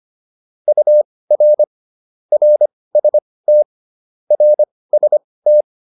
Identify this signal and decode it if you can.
Morse code, heard as a keyed tone, about 25 wpm, spaced as for 18 wpm, 600 Hz. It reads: UR RST RST